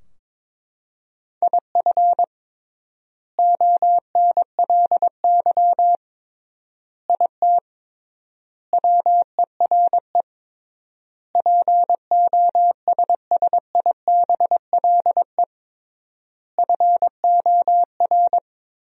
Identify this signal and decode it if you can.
Morse code, heard as a keyed tone, about 21 wpm, 700 Hz. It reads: IF ONLY IT WERE POSSIBLE FOR
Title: 3x Word-Spacing - Sentences from Top 500 Words - 22wpm